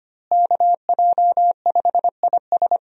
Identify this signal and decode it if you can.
Morse code, heard as a keyed tone, about 25 wpm, 700 Hz. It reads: KJ5IS
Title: USA Call Signs - 5 Characters - Repeated 2x - 25wpm